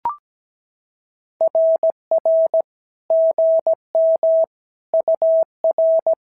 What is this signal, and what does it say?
Morse code, heard as a keyed tone, about 16 wpm, 650 Hz. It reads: RR GM UR
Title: SOTA QSO Practice 17wpm